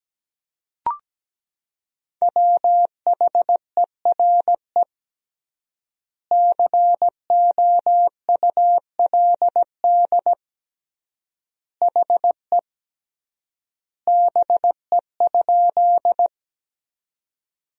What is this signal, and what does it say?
Morse code, heard as a keyed tone, about 17 wpm, 700 Hz. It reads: WHERE COULD HE BE?
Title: 3x Word-Spacing - Sentences from Top 200 Words - 17wpm